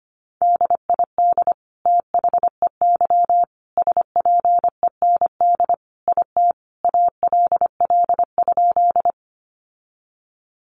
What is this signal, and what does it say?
Morse code, heard as a keyed tone, about 25 wpm, 700 Hz. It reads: DID THEY SPEND IT ALL?
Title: Sentences from Top 800 Words 25wpm